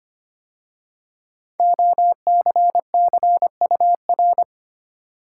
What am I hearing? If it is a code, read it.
Morse code, heard as a keyed tone, about 25 wpm, 700 Hz. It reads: OCCUR
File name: 5 Letter Words - First 3 Letters Given - 25wpm